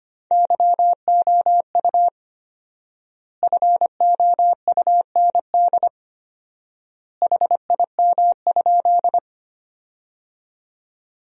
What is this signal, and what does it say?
Morse code, heard as a keyed tone, about 25 wpm, 700 Hz. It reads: YOU FOUND HIM?
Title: 4x Word-Spacing - Sentences from Top 300 Words - 25wpm